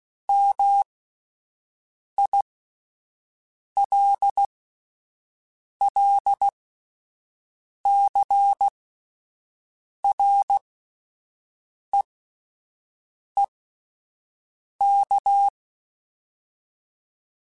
Morse code, heard as a keyed tone, about 16 wpm, spaced as for 5 wpm, 800 Hz. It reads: MILLCREEK